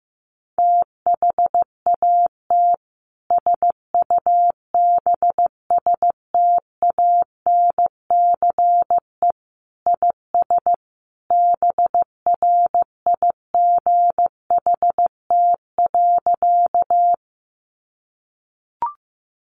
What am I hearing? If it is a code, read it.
Morse code, heard as a keyed tone, about 14 wpm, 700 Hz. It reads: THAT SUBSTANCE IS BRIGHT.
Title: Sentences from Top 1500 Words 15wpm